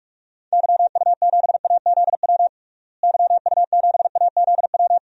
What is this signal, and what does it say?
Morse code, heard as a keyed tone, about 41 wpm, 700 Hz. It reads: YU7AZW YU7AZW
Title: DXCC-100 Call Signs - 6 Characters - Repeated 3x - 45wpm